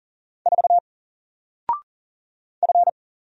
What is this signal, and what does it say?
Morse code, heard as a keyed tone, about 40 wpm, 700 Hz. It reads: 4 F
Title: Single Letter-Number Alt 40wpm